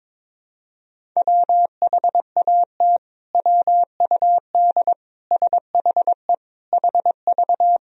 Morse code, heard as a keyed tone, about 22 wpm, 700 Hz. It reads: WHAT WUD SHE HV
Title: Sentences from Top 100 Words and Abbreviations 22wpm